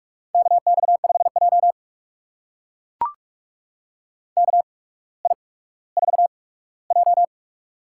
Morse code, heard as a keyed tone, about 44 wpm, spaced as for 19 wpm, 690 Hz. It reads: KX5J KI4J
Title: USA Call Signs - 4 Characters - 8x Character Spacing - 45wpm